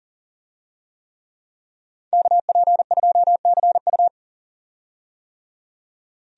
Morse code, heard as a keyed tone, about 40 wpm, 700 Hz. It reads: KP2CU
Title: DXCC-100 Call Signs - 5 Characters - 40wpm